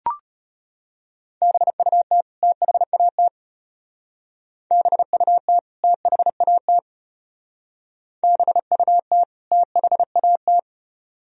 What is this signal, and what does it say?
Morse code, heard as a keyed tone, about 32 wpm, 700 Hz. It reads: BUT THAT BUT THAT BUT THAT